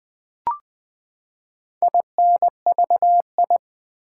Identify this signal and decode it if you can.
Morse code, heard as a keyed tone, about 20 wpm, 700 Hz. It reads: INVI
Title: Build A Word - 4 Letters At A Time - 10-Letter Words - 20wpm